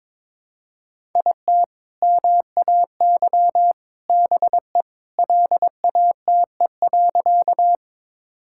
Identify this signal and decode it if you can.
Morse code, heard as a keyed tone, about 22 wpm, 700 Hz. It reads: IT MAY BE LATE.
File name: practice / 22wpm / Sentences with 2-Word Phrases - 22wpm